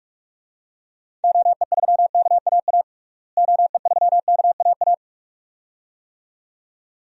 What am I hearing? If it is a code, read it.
Morse code, heard as a keyed tone, about 46 wpm, 700 Hz. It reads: OE3KAA OE3KAA
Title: DXCC-100 Call Signs - 6 Characters - Repeated 2x - 45wpm